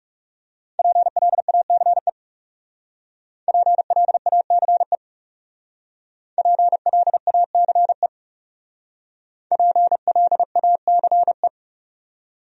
Morse code, heard as a keyed tone, about 38 wpm, 700 Hz. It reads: PLACE PLACE PLACE PLACE